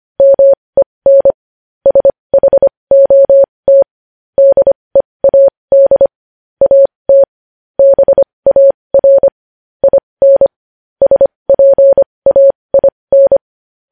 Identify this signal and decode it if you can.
Morse code, heard as a keyed tone, about 25 wpm, spaced as for 20 wpm, 550 Hz. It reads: MEN SHOT DEAD AT BAR IN SPAIN
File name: podcast / News headlines-1x-25x20